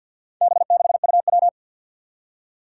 Morse code, heard as a keyed tone, about 51 wpm, 700 Hz. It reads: B6UW
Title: Four Challenging Letter-Number Combinations 50wpm